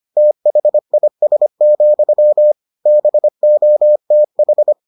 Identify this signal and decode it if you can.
Morse code, heard as a keyed tone, about 25 wpm, 600 Hz. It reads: THIS, BOTH